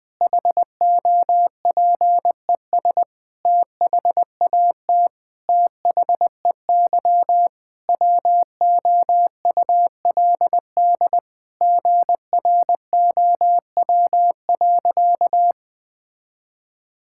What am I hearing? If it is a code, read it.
Morse code, heard as a keyed tone, about 21 wpm, 700 Hz. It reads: HOPES THAT THEY WOULD GROW.
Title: Sentences with 9-Word Phrases - 20wpm